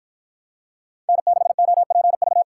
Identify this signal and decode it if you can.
Morse code, heard as a keyed tone, about 50 wpm, 700 Hz. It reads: N6OPV